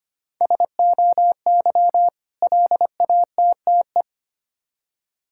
Morse code, heard as a keyed tone, about 25 wpm, 700 Hz. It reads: SOY LATTE
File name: practice / 25wpm / Difficult Words with Hints - 25wpm